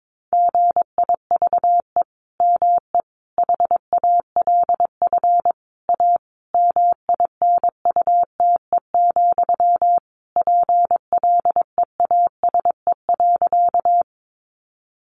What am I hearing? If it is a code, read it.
Morse code, heard as a keyed tone, about 22 wpm, 700 Hz. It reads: GIVE ME HALF A MINUTE, PLEASE.
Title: Sentences from Top 1500 Words 22wpm